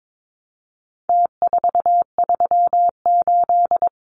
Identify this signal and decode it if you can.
Morse code, heard as a keyed tone, about 22 wpm, 700 Hz. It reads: T438
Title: Numbers - Contest Style - 4-Digit 22wpm